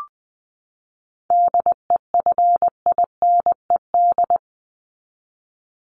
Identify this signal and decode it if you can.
Morse code, heard as a keyed tone, about 20 wpm, 700 Hz. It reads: DEFINED
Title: ED Suffix - Top 100 Words with Base Words 20wpm